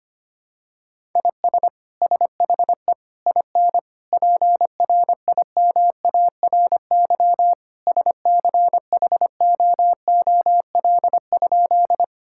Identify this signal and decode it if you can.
Morse code, heard as a keyed tone, about 25 wpm, 700 Hz. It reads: IS SHE IN PRIMARY SCHOOL?
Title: Sentences from Top 1000 Words - Review Part 5 - 25wpm